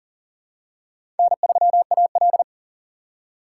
Morse code, heard as a keyed tone, about 37 wpm, 690 Hz. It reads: N3AL
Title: USA Call Signs - 4 Characters - 2x Character Spacing - 40wpm